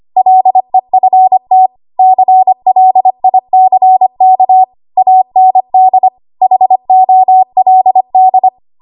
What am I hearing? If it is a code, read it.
Morse code, heard as a keyed tone, about 25 wpm, 750 Hz. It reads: LEFT CLICK AND HOLD
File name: Archive / 25 WPM / 25WPM